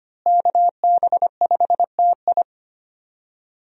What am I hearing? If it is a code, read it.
Morse code, heard as a keyed tone, about 25 wpm, 700 Hz. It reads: KB5TI
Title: USA Call Signs - 5 Characters - Repeated 2x - 25wpm